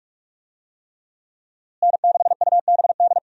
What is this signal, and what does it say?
Morse code, heard as a keyed tone, about 42 wpm, 700 Hz. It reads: N6UBD
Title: USA Call Signs - 5 Characters - Repeated 3x - 45wpm